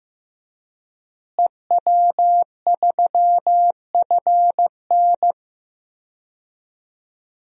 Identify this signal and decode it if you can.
Morse code, heard as a keyed tone, about 15 wpm, 700 Hz. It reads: EW3FN
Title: DXCC-100 Call Signs - 5 Characters - Repeated 2x - 15wpm